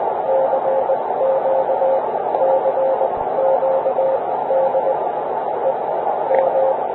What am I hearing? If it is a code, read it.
Morse code, heard as a keyed tone, about 20 wpm, spaced as for 13 wpm, 530 Hz, with noise. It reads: CQ CQ DE M